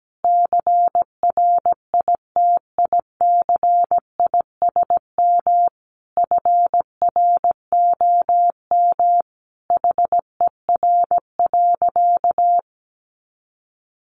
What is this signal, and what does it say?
Morse code, heard as a keyed tone, about 17 wpm, 700 Hz. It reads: CRITICISM FROM HER.
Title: Sentences from Top 1300 Words 17wpm